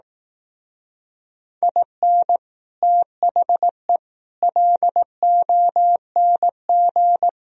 Morse code, heard as a keyed tone, about 18 wpm, 700 Hz. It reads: IN THE LONG
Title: Speed-Racing - Sets of 4 Words 15wpm